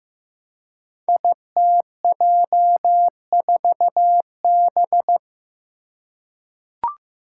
Morse code, heard as a keyed tone, about 15 wpm, 700 Hz. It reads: ITJ4B
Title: Five Challenging Letter-Number Combinations 15wpm